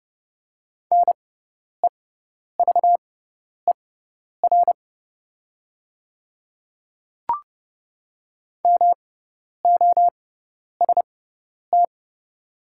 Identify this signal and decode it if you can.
Morse code, heard as a keyed tone, about 30 wpm, spaced as for 10 wpm, 700 Hz. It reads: NEVER MOST
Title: 6x Character Spacing - Top 300 Words - 30wpm